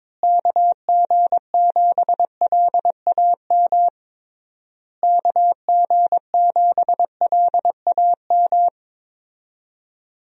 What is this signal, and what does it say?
Morse code, heard as a keyed tone, about 22 wpm, 700 Hz. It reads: KG7LAM KG7LAM
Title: USA Call Signs - 6 Characters - Repeated 3x - 22wpm